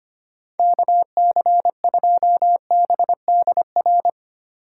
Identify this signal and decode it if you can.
Morse code, heard as a keyed tone, about 31 wpm, 700 Hz. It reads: KC2BDR
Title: Speed Racing - USA 2x3 Call Signs 25wpm